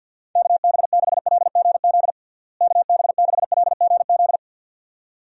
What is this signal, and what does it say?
Morse code, heard as a keyed tone, about 48 wpm, 700 Hz. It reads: KB6LGZ KB6LGZ
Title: USA Call Signs - 6 Characters - Repeated 2x - 50wpm